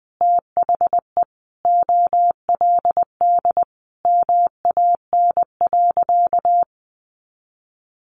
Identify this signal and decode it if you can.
Morse code, heard as a keyed tone, about 20 wpm, 700 Hz. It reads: THE OLD MAN.
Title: Sentences from Top 1500 Words 20wpm